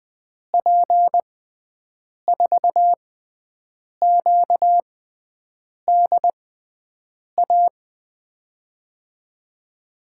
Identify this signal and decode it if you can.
Morse code, heard as a keyed tone, about 20 wpm, spaced as for 7 wpm, 700 Hz. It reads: P4QDA
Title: USA Call Signs - 6x Character Spacing - 20wpm